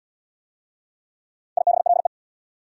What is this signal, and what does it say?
Morse code, heard as a keyed tone, about 68 wpm, 700 Hz. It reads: IBLE